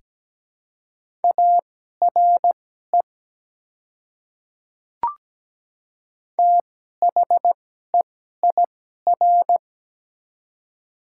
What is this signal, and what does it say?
Morse code, heard as a keyed tone, about 17 wpm, spaced as for 12 wpm, 690 Hz. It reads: ARE THEIR
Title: 2x Character Spacing - Top 100 Words - 17wpm